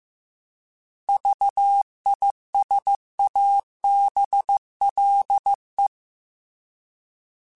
Morse code, heard as a keyed tone, about 15 wpm, 800 Hz. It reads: VISABLE
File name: Alphabetical / Alphabetical mp3s / Vs @13